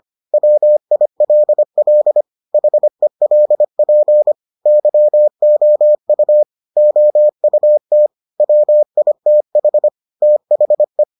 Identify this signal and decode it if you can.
Morse code, heard as a keyed tone, about 25 wpm, 600 Hz. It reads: WILL HELP YOU OUT WITH THE